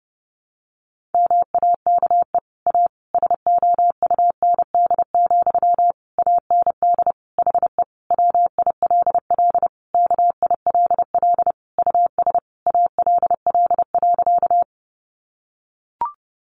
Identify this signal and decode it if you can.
Morse code, heard as a keyed tone, about 30 wpm, 700 Hz. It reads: MAKE A SOUND, AND HE WILL KILL US ALL.
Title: Sentences from Top 1400 Words 30wpm